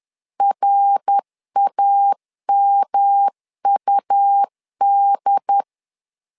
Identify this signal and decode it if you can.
Morse code, heard as a keyed tone, about 10 wpm, 800 Hz. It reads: RAMUD
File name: LEZIONI / 3 MWRO / 3 MWR 10wpm 1952